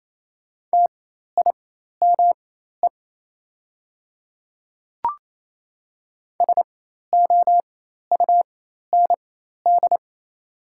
Morse code, heard as a keyed tone, about 27 wpm, spaced as for 13 wpm, 690 Hz. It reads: TIME SOUND
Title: 4x Character Spacing - Top 200 Words - 28wpm